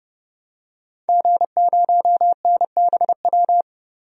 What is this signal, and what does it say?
Morse code, heard as a keyed tone, about 30 wpm, 700 Hz. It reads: G0NBW
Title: DXCC-100 Call Signs - 5 Characters - Repeated 3x - 30wpm